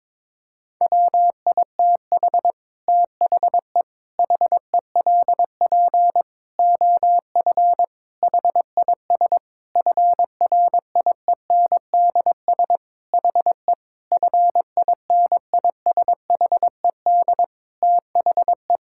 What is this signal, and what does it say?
Morse code, heard as a keyed tone, about 22 wpm, 700 Hz. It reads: WITH THE HELP OF HIS FRIENDS HE FINISHED THE
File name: practice / 22wpm / Sentences with 4-Word Phrases - 22wpm